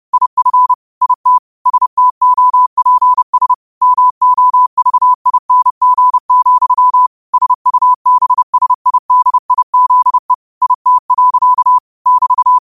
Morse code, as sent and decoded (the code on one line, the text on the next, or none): .. ..-. / .. - / ... - --- .--. ... / -- --- ...- .. -. --. --..-- / ... ..- -... ... .. -.. .. --.. / .. - .-.-.- / -...-
IF IT STOPS MOVING, SUBSIDIZE IT. <BT>